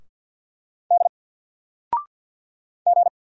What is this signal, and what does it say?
Morse code, heard as a keyed tone, about 48 wpm, 700 Hz. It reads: D G
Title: Single Letter-Number Alt 50wpm